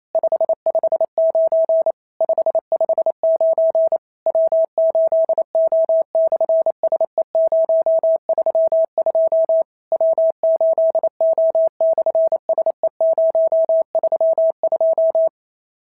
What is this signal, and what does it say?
Morse code, heard as a keyed tone, about 28 wpm, 650 Hz. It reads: 559 559 W8O/SE032 W8O/SE032
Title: SOTA QSO Practice 28wpm